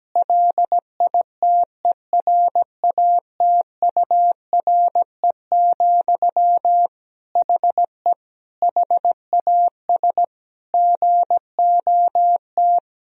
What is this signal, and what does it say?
Morse code, heard as a keyed tone, about 17 wpm, 700 Hz. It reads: LITERATURE, HE HAS GOT
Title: Sentences from Top 1000 Words - Review Part 3 - 17wpm